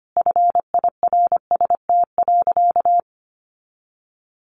Morse code, heard as a keyed tone, about 25 wpm, 700 Hz. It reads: FIRST.
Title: Sentences from Top 1400 Words 25wpm